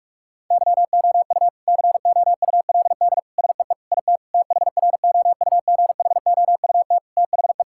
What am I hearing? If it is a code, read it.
Morse code, heard as a keyed tone, about 46 wpm, 700 Hz. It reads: YOU COULD SEE IT THROUGHOUT THE
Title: Sentences from Top 1000 Words - Review Part 5 - 45wpm